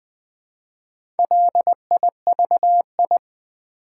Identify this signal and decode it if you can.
Morse code, heard as a keyed tone, about 21 wpm, 700 Hz. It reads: LIVI
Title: Build A Word - 2 Letters At A Time - 6-Letter Words - 20wpm